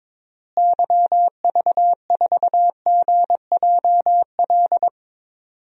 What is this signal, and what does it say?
Morse code, heard as a keyed tone, about 21 wpm, 700 Hz. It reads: YV4GJL
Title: DXCC-100 Call Signs - 6 Characters - Repeated 2x - 22wpm